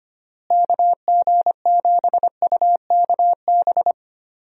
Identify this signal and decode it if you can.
Morse code, heard as a keyed tone, about 25 wpm, 700 Hz. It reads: KG7UKB
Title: USA Call Signs - 6 Characters - Repeated 3x - 25wpm